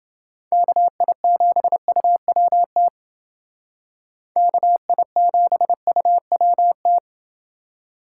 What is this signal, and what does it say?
Morse code, heard as a keyed tone, about 28 wpm, 700 Hz. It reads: KI7UWT KI7UWT